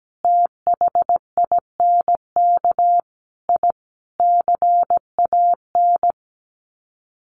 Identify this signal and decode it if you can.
Morse code, heard as a keyed tone, about 17 wpm, 700 Hz. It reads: THINK I CAN
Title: Sets of 5 Words Encore2 17wpm